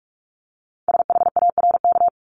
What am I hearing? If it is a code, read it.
Morse code, heard as a keyed tone, about 46 wpm, 700 Hz. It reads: SHARK
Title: Multi-letter Phonemes 45wpm